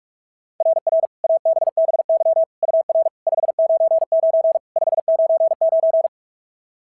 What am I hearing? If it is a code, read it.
Morse code, heard as a keyed tone, about 44 wpm, 660 Hz. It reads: RR ABBY UR 599 599
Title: SOTA QSO Practice 45wpm